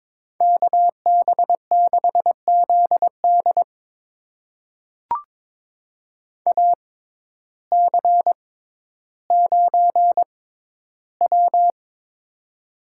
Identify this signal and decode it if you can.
Morse code, heard as a keyed tone, about 22 wpm, spaced as for 12 wpm, 700 Hz. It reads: KB6ZD AC9W
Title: USA Call Signs - 6x Character Spacing - 22wpm